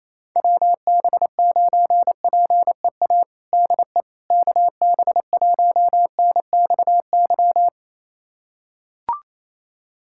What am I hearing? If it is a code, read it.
Morse code, heard as a keyed tone, about 28 wpm, 700 Hz. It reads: WB9PEA DE KB1NXY
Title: USA Callsign DE Callsign - 6 Characters - 28wpm